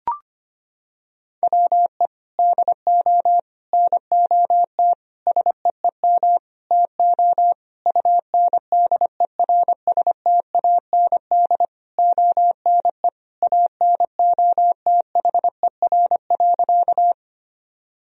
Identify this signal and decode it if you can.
Morse code, heard as a keyed tone, about 25 wpm, 700 Hz. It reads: WE DO NOT SEEM TO UNDERSTAND ONE ANOTHER.
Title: Sentences from Top 1000 Words - Review Part 1 - 25wpm